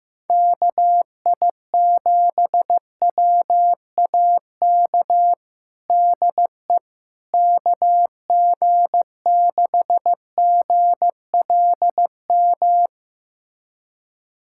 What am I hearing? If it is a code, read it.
Morse code, heard as a keyed tone, about 15 wpm, 700 Hz. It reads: KI7WAK DE KG6GLM
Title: USA Callsign DE Callsign - 6 Characters - 15wpm